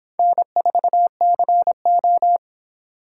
Morse code, heard as a keyed tone, about 26 wpm, 700 Hz. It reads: N4CO